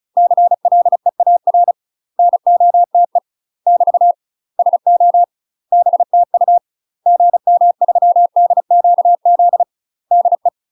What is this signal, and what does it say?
Morse code, heard as a keyed tone, about 36 wpm, 700 Hz. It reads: CLEAR NOTE <BT> SO BTU GM3DQZ DE
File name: UK MASTER QSOs TRAINING / 35WPM / UK MASTER QSOs 0017